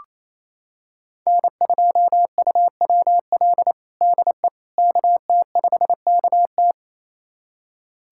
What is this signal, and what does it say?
Morse code, heard as a keyed tone, about 28 wpm, 700 Hz. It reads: N2UWL DE KT5KT
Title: USA Callsign DE Callsign - 5 Characters - 28wpm